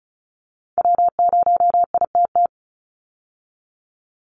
Morse code, heard as a keyed tone, about 34 wpm, 700 Hz. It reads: W0ITT
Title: Call Signs Repeated 3x - 35wpm